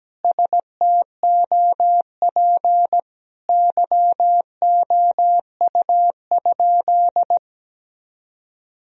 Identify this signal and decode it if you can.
Morse code, heard as a keyed tone, about 17 wpm, 700 Hz. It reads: STOP YOU?
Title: Sentences from Top 1000 Words - Review Part 1 - 17wpm